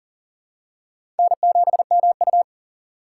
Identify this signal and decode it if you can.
Morse code, heard as a keyed tone, about 37 wpm, 700 Hz. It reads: N7MU
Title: USA Call Signs - 2x Character Spacing - 40wpm